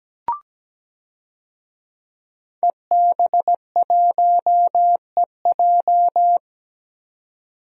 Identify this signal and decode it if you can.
Morse code, heard as a keyed tone, about 17 wpm, 700 Hz. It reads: EB1EJ
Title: DXCC-100 Call Signs - 5 Characters - 17wpm